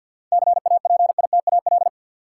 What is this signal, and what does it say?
Morse code, heard as a keyed tone, about 53 wpm, 700 Hz. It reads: CAPITAL
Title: Sentences from Top 1000 Words - New Word Drill - 50wpm